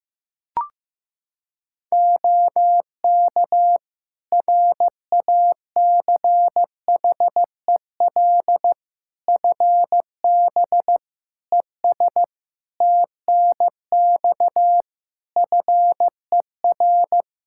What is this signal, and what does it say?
Morse code, heard as a keyed tone, about 16 wpm, 700 Hz. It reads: OK RACHEL FB ES TNX FER
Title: K1USN SST QSO Practice 15wpm